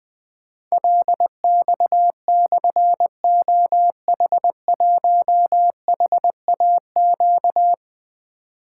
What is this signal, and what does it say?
Morse code, heard as a keyed tone, about 20 wpm, 700 Hz. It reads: LX/OH1HAQ
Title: Difficult Call Signs 20wpm